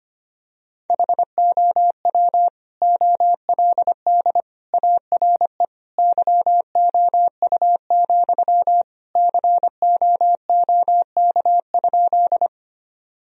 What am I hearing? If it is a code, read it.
Morse code, heard as a keyed tone, about 25 wpm, 700 Hz. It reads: HOW OLD ARE YOU, COOK?
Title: Sentences from Top 1000 Words - Review Part 4 - 25wpm